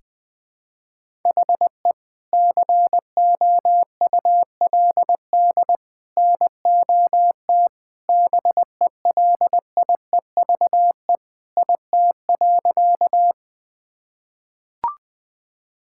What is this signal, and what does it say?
Morse code, heard as a keyed tone, about 21 wpm, 700 Hz. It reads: HE COULD NOT BELIEVE IT.
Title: Sentences with 3-Word Phrases - Spoken Beginning - 20wpm